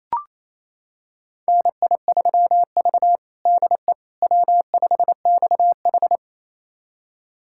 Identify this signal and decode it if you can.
Morse code, heard as a keyed tone, about 28 wpm, 700 Hz. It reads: NI3V DE W5XH
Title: USA Callsign DE Callsign - 4 Characters - 28wpm